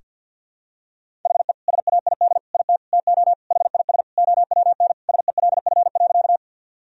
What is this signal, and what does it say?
Morse code, heard as a keyed tone, about 46 wpm, 700 Hz. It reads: HE SAID IT TO HIS OWN SELF.
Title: Sentences from Top 1000 Words - Review Part 1 - 50wpm